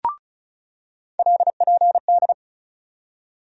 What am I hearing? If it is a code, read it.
Morse code, heard as a keyed tone, about 34 wpm, 700 Hz. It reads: LPD
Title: USA Call Sign Postfixes - 3 Letters 35wpm